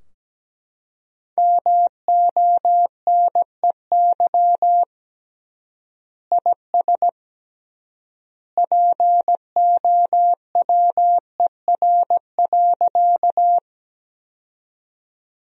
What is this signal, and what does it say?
Morse code, heard as a keyed tone, about 17 wpm, 700 Hz. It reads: MONEY IS POWER.
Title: 3x Word-Spacing - Sentences from Top 500 Words - 17wpm